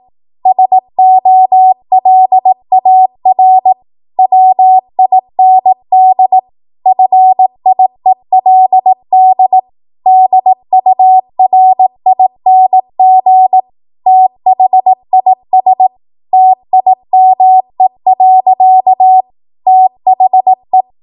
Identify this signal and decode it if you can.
Morse code, heard as a keyed tone, about 18 wpm, 750 Hz. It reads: SOLAR WIND FIELD DURING THIS TIME. THE
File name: Morse / 18WPM